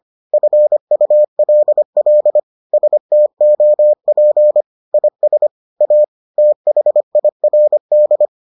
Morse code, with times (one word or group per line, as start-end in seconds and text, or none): 0.3-2.4 FULL
2.7-4.6 STOP
4.9-5.5 IS
5.8-6.0 A
6.4-8.3 THIRD